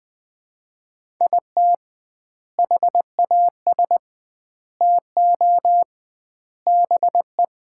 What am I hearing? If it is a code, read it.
Morse code, heard as a keyed tone, about 21 wpm, 700 Hz. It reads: IT HAS TO BE
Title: 2x Word-Spacing - Sets of 4 Words - 20wpm